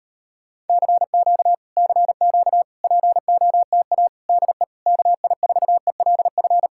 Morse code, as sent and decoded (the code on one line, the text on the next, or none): -.-. --.- / -.-. --.- / .--. --- - .- / -.. / -.- .. ....- . .-.. ..-.
CQ CQ POTA DE KI4ELF